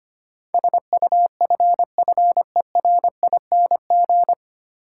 Morse code, heard as a keyed tone, about 25 wpm, 700 Hz. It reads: SUFFERING
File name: practice / 25wpm / 9 Letter Words - First 3 Letters Given - 25wpm